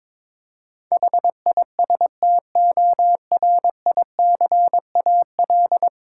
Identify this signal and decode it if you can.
Morse code, heard as a keyed tone, about 22 wpm, 700 Hz. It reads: HISTORICAL
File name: Difficult Words with Hints - 22wpm